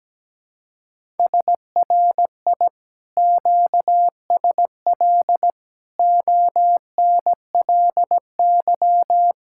Morse code, heard as a keyed tone, about 17 wpm, 700 Hz. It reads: SRI QSL ONLY
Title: Speed-Racing Common QSO Phrases - 17wpm